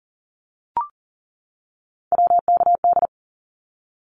Morse code, heard as a keyed tone, about 42 wpm, 700 Hz. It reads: WKD
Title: Abbreviations 40wpm